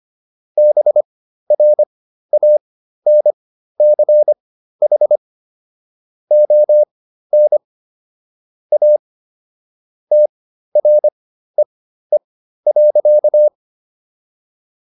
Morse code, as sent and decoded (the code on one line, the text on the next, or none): -... .-. .- -. -.-. .... / --- -. / .- / - .-. . .-.-.-
BRANCH ON A TREE.